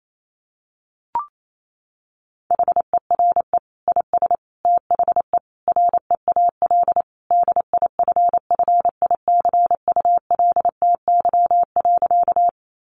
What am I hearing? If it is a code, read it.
Morse code, heard as a keyed tone, about 28 wpm, 700 Hz. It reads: HERE IS THE REAL DIFFICULTY.
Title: Sentences from Top 1500 Words 28wpm